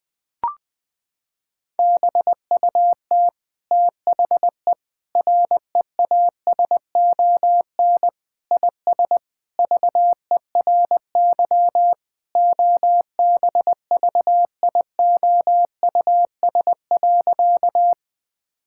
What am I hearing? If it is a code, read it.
Morse code, heard as a keyed tone, about 21 wpm, 700 Hz. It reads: BUT THE REASON IS VERY OBVIOUS.
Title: Sentences from Top 1600 Words 20wpm